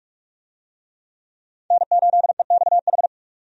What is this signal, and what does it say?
Morse code, heard as a keyed tone, about 46 wpm, 700 Hz. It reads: N8EXH
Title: USA Call Signs - 5 Characters - 45wpm